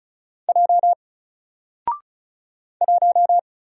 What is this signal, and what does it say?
Morse code, heard as a keyed tone, about 34 wpm, 700 Hz. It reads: J 1
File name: Single-Letter-Number-35wpm